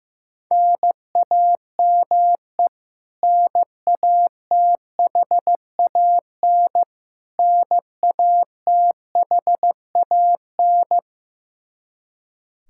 Morse code, heard as a keyed tone, about 16 wpm, 700 Hz. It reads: NAME NATHAN NATHAN
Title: Practice QSO - Exchange 1 - Beginning - 15wpm